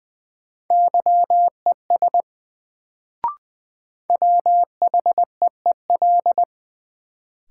Morse code, heard as a keyed tone, about 20 wpm, 700 Hz. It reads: YES WHEEL
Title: Sentences from Top 500 Words - New Word Drill - 20wpm